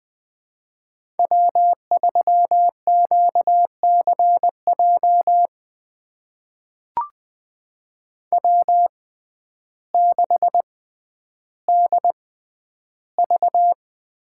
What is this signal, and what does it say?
Morse code, heard as a keyed tone, about 20 wpm, spaced as for 11 wpm, 700 Hz. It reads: W3QCJ W6DV
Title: USA Call Signs - 5 Characters - 6x Character Spacing - 20wpm